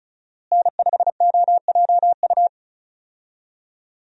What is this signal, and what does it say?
Morse code, heard as a keyed tone, about 34 wpm, 700 Hz. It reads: N5OJU
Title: USA Call Signs - 5 Characters - Repeated 2x - 35wpm